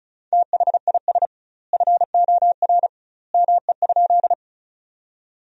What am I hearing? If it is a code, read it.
Morse code, heard as a keyed tone, about 36 wpm, 700 Hz. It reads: THIS FOR ME?
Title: 2x Word-Spacing - Sentences from Top 200 Words - 35wpm